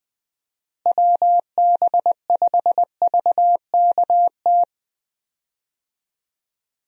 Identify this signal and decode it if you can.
Morse code, heard as a keyed tone, about 20 wpm, 700 Hz. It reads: WB5VKT
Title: USA Call Signs - 6 Characters - Repeated 3x - 20wpm